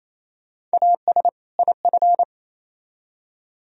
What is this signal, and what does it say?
Morse code, heard as a keyed tone, about 37 wpm, 700 Hz. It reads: AS IF